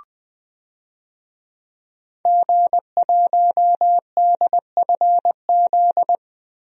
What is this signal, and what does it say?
Morse code, heard as a keyed tone, about 20 wpm, 700 Hz. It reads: G1DFZ
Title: DXCC-100 Call Signs - 5 Characters - Repeated 3x - 20wpm